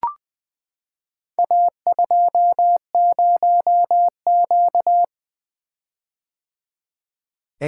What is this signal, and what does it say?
Morse code, heard as a keyed tone, about 20 wpm, 700 Hz. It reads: A20Q
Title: Four Letter-Number Combinations - 20wpm